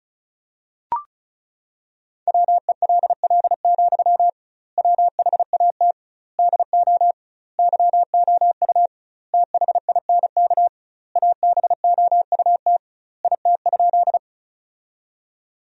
Morse code, heard as a keyed tone, about 34 wpm, 700 Hz. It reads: WELL, WHAT DO YOU THINK ABOUT IT?
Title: 2x Word-Spacing - Sentences from Top 200 Words - 35wpm